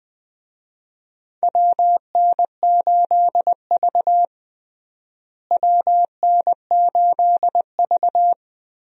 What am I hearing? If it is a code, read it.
Morse code, heard as a keyed tone, about 20 wpm, 700 Hz. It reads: WN8V WN8V
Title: USA Call Signs - 4 Characters - Repeated 2x - 20wpm